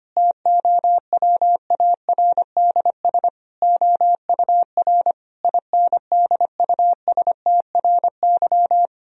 Morse code, heard as a keyed tone, about 25 wpm, 700 Hz. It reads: TOWARDS OUR INDUSTRY
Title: Sentences from Top 1000 Words - Review Part 4 - 25wpm